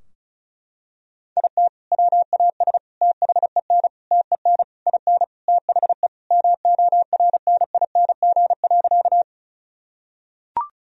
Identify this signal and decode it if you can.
Morse code, heard as a keyed tone, about 36 wpm, 700 Hz. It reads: IT WAS THEN TEN IN THE MORNING.
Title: Sentences from Top 1000 Words - Review Part 2 - 35wpm